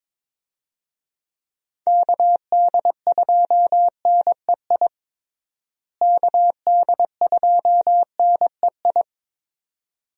Morse code, heard as a keyed tone, about 22 wpm, 700 Hz. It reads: KD2NEI KD2NEI
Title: USA Call Signs - 6 Characters - Repeated 2x - 22wpm